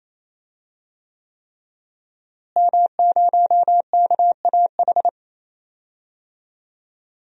Morse code, heard as a keyed tone, about 27 wpm, 700 Hz. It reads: M0KAH
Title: DXCC-100 Call Signs - 5 Characters - Repeated 3x - 28wpm